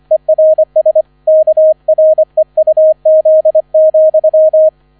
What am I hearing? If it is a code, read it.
Morse code, heard as a keyed tone, about 25 wpm, 630 Hz, with radio noise. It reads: ERS KREUZ,